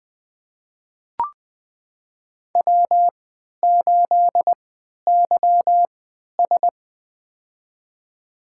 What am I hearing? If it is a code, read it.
Morse code, heard as a keyed tone, about 20 wpm, spaced as for 11 wpm, 700 Hz. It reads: W8YS
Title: USA Call Signs - 4 Characters - 3x Character Spacing - 20wpm